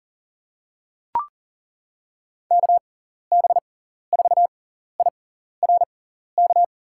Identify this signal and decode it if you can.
Morse code, heard as a keyed tone, about 37 wpm, spaced as for 14 wpm, 700 Hz. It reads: KB4IRK
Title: USA Call Signs - 6 Characters - 6x Character Spacing - 40wpm